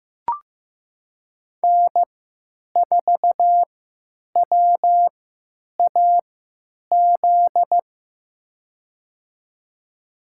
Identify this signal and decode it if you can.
Morse code, heard as a keyed tone, about 15 wpm, spaced as for 9 wpm, 700 Hz. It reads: N4WAZ
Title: USA Call Signs - 5 Characters - 3x Character Spacing - 15wpm